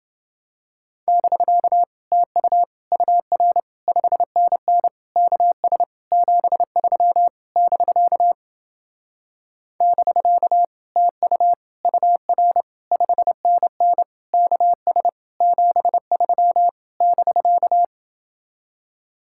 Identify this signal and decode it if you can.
Morse code, heard as a keyed tone, about 28 wpm, 700 Hz. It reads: <BK> TU UR 5NN KS 73 <BK> <BK> TU UR 5NN KS 73 <BK>